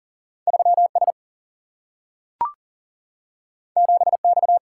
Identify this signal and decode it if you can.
Morse code, heard as a keyed tone, about 40 wpm, 700 Hz. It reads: 3 S 7 X
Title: Single-Letter Single-Number 40wpm